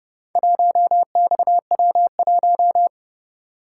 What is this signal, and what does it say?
Morse code, heard as a keyed tone, about 27 wpm, 700 Hz. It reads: 1XW1